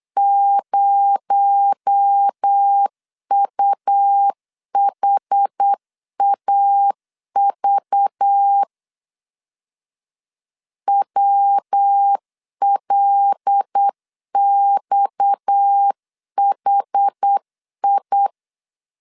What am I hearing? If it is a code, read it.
Morse code, heard as a keyed tone, about 8 wpm, 800 Hz. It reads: 0UHAV WLXHI
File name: CW / DET / 1 6 1500C S C 8wpm 5219